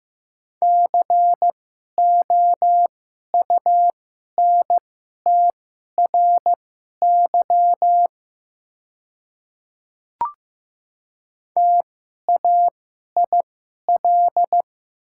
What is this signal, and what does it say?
Morse code, heard as a keyed tone, about 15 wpm, spaced as for 11 wpm, 700 Hz. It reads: COUNTRY TAIL
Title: 2x Character Spacing - Top 500 Words - 15wpm